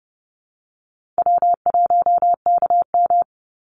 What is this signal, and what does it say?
Morse code, heard as a keyed tone, about 30 wpm, 700 Hz. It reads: W1KM
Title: Easy Call Signs Repeated 2x - 30wpm